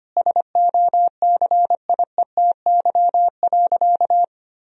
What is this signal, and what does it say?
Morse code, heard as a keyed tone, about 25 wpm, 700 Hz. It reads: SOCIETY.
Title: Sentences from Top 1000 Words - Review Part 4 - 25wpm